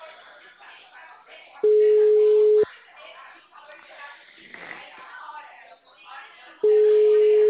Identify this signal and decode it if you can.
Ringback tone (Europe, 425 Hz), with speech.
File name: out- -2015- - - .468